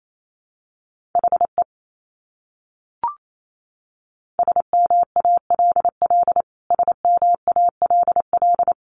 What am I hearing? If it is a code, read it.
Morse code, heard as a keyed tone, about 28 wpm, 700 Hz. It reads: HE SMALL SMALL
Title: Top 300 Words Repeated 28wpm